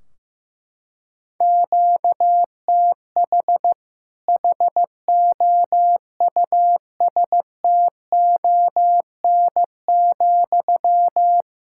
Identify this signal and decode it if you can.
Morse code, heard as a keyed tone, about 15 wpm, 700 Hz. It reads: QTH HOUSTON,
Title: Practice QSO - Exchange 1 - Beginning - 15wpm